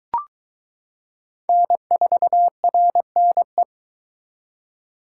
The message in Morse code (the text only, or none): N4RNE